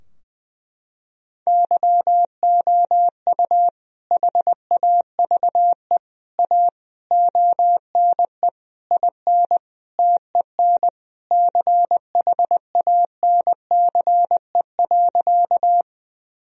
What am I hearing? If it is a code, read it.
Morse code, heard as a keyed tone, about 20 wpm, 700 Hz. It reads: YOU HAVE A ONE IN TEN CHANCE.
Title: Sentences from Top 1000 Words - Review Part 2 - 20wpm